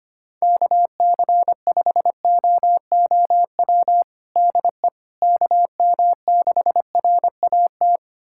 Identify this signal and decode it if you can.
Morse code, heard as a keyed tone, about 25 wpm, 700 Hz. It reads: KC5OOW DE KM6RAT
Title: USA Callsign DE Callsign - 6 Characters - 25wpm